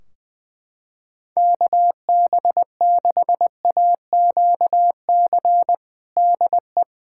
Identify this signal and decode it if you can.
Morse code, heard as a keyed tone, about 21 wpm, 700 Hz. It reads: KB6AQC DE
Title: Practice QSO - Exchange 4 - Ending - 20wpm